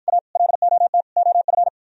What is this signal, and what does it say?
Morse code, heard as a keyed tone, about 53 wpm, 700 Hz. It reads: A LOT OF